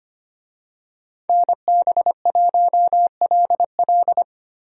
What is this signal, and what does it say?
Morse code, heard as a keyed tone, about 25 wpm, 700 Hz. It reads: NB1LL
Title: USA Call Signs - 5 Characters - Repeated 3x - 25wpm